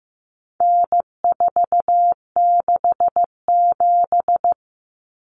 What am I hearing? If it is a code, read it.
Morse code, heard as a keyed tone, about 15 wpm, 700 Hz. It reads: N467
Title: Common Cut Numbers - 4-Digit 15wpm